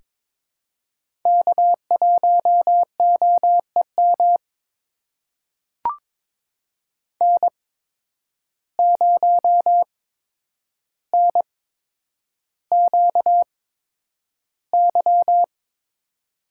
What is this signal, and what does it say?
Morse code, heard as a keyed tone, about 23 wpm, spaced as for 10 wpm, 700 Hz. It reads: K1OEM N0NQY
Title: USA Call Signs - 5 Characters - 8x Character Spacing - 22wpm